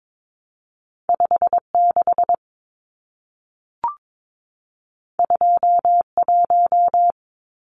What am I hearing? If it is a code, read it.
Morse code, heard as a keyed tone, about 22 wpm, 700 Hz. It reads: 56 21
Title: Numbers - 2-Digit 22wpm